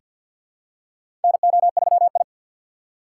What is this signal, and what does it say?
Morse code, heard as a keyed tone, about 47 wpm, 700 Hz. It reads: NO3I
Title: USA Call Signs - 3x Character Spacing - 50wpm